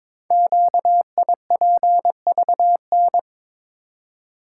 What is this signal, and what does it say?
Morse code, heard as a keyed tone, about 22 wpm, 700 Hz. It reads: QIPVN
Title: Code Groups - 5 Letter:Numbers - 22wpm